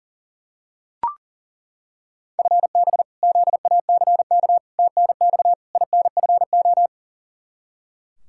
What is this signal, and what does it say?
Morse code, heard as a keyed tone, about 40 wpm, 700 Hz. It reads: FB ZACK TNX INFO
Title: Practice QSO - Exchange 3 - More Info - 40wpm